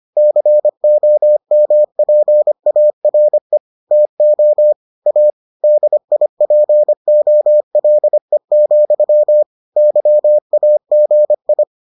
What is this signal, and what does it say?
Morse code, heard as a keyed tone, about 25 wpm, 600 Hz. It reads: COMPARE TO A DIPOLE, YAGI